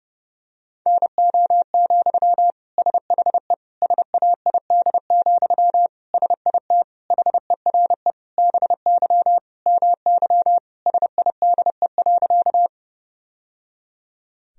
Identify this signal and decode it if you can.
Morse code, heard as a keyed tone, about 30 wpm, 700 Hz. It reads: NO, SHE SAID, SIT HERE BY MY SIDE.
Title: Sentences from Top 1000 Words - Review Part 1 - 30wpm